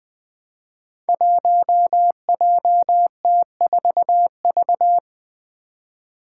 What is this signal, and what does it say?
Morse code, heard as a keyed tone, about 21 wpm, 700 Hz. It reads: 1JT4V
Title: Five Challenging Letter-Number Combinations 20wpm